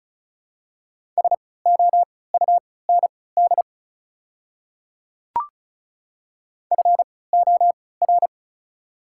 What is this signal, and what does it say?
Morse code, heard as a keyed tone, about 36 wpm, spaced as for 20 wpm, 700 Hz. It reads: SOUND FOR
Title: 3x Character Spacing - Top 100 Words - 35wpm